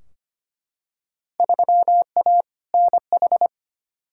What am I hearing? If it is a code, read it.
Morse code, heard as a keyed tone, about 25 wpm, 700 Hz. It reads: 3A NH
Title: Field Day Exchanges - 25wpm